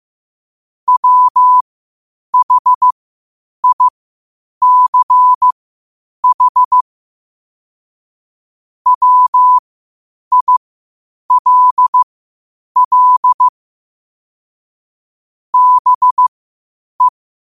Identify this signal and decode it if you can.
Morse code, heard as a keyed tone, about 15 wpm, spaced as for 8 wpm, 1000 Hz. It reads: WHICH WILL BE
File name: qotdfiles / 8wpm